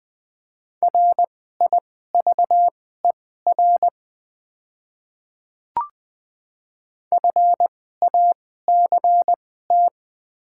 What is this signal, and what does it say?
Morse code, heard as a keyed tone, about 20 wpm, spaced as for 14 wpm, 700 Hz. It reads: RIVER FACT
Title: 2x Character Spacing - Top 500 Words - 20wpm